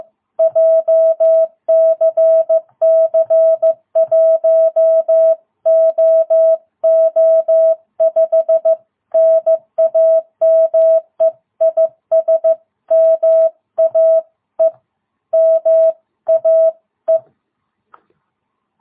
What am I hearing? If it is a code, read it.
Morse code, heard as a keyed tone, about 14 wpm, 660 Hz. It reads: JCC1OO5 NAME IS MAE MAE
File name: jh8cbh / 2022cw